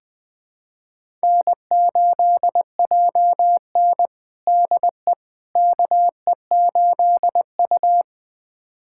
Morse code, heard as a keyed tone, about 20 wpm, 700 Hz. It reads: N8JN DE KE8U
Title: USA Callsign DE Callsign - 4 Characters - 20wpm